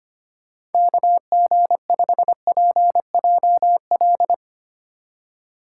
Morse code, heard as a keyed tone, about 31 wpm, 700 Hz. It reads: KG5PJL
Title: Speed Racing - USA 2x3 Call Signs 25wpm